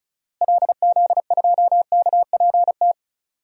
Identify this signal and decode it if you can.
Morse code, heard as a keyed tone, about 36 wpm, 700 Hz. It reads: LZ2KPT
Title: DXCC-100 Call Signs - 6 Characters - Repeated 3x - 35wpm